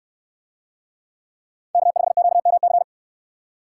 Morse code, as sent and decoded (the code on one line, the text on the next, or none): -.- ..... --.- -- -.-.
K5QMC